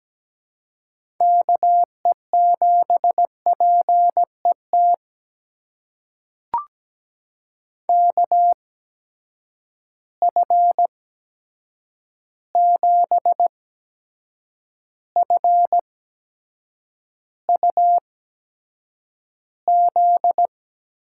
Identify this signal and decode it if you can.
Morse code, heard as a keyed tone, about 17 wpm, spaced as for 7 wpm, 700 Hz. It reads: KE7PET KF7FUZ
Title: USA Call Signs - 6 Characters - 8x Character Spacing - 17wpm